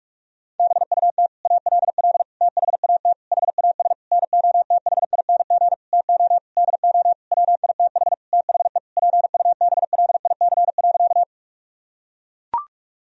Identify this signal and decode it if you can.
Morse code, heard as a keyed tone, about 41 wpm, 700 Hz. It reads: BUT ALL THAT HAS NOTHING TO DO WITH THE PUBLIC.
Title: Sentences from Top 1000 Words - Review Part 3 - 45wpm